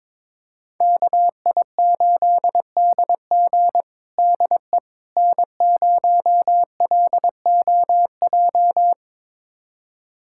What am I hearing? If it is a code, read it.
Morse code, heard as a keyed tone, about 22 wpm, 700 Hz. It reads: KI8DG DE N0LOJ
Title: USA Callsign DE Callsign - 5 Characters - 22wpm